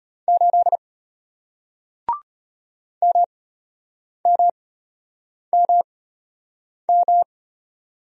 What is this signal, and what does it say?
Morse code, heard as a keyed tone, about 33 wpm, spaced as for 8 wpm, 700 Hz. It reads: 8 MMMM